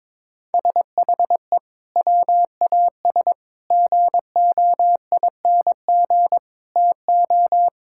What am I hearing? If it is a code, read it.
Morse code, heard as a keyed tone, about 22 wpm, 700 Hz. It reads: SHE WAS GOING TO
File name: practice / 22wpm / Sentences with 3-Word Phrases - 22wpm